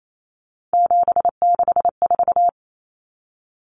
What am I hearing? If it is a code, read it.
Morse code, heard as a keyed tone, about 28 wpm, 700 Hz. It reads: 764
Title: Numbers - 3-Digit 28wpm